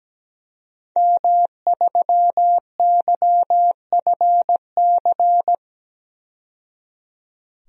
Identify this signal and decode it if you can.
Morse code, heard as a keyed tone, about 17 wpm, 700 Hz. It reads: M3YFC
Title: DXCC-100 Call Signs - 5 Characters - 17wpm